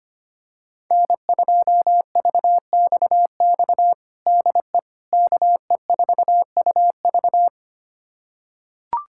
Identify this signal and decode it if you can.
Morse code, heard as a keyed tone, about 25 wpm, 700 Hz. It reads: N2VXX DE KE4UV
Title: USA Callsign DE Callsign - 5 Characters - 25wpm